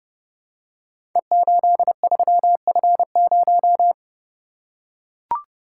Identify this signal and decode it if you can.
Morse code, heard as a keyed tone, about 28 wpm, 700 Hz. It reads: E83F0
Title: Code Groups - 5 Letter:Numbers - 30wpm